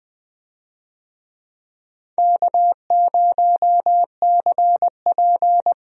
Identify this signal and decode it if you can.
Morse code, heard as a keyed tone, about 20 wpm, 700 Hz. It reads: K0CP
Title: USA Call Signs - 4 Characters - Repeated 2x - 20wpm